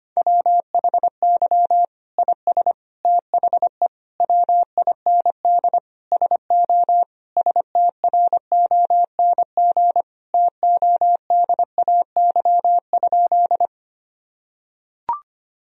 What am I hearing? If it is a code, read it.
Morse code, heard as a keyed tone, about 25 wpm, 700 Hz. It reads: WHY IS THE WIND SO STRONG TODAY?
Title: Sentences from Top 1000 Words - Review Part 2 - 25wpm